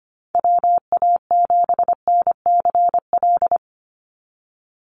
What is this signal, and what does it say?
Morse code, heard as a keyed tone, about 25 wpm, 700 Hz. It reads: WA7NCL
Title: NA SOTA Activator Callsigns 25wpm